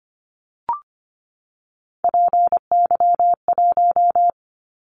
Morse code, heard as a keyed tone, about 25 wpm, 700 Hz. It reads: PY1
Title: Callsign Prefixes for Top 100 DXCC - 25wpm